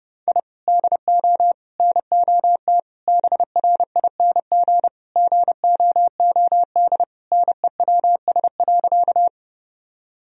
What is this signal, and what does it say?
Morse code, heard as a keyed tone, about 29 wpm, 700 Hz. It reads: I DO NOT BRING GOOD NEWS.
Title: Sentences from Top 1000 Words - Review Part 3 - 30wpm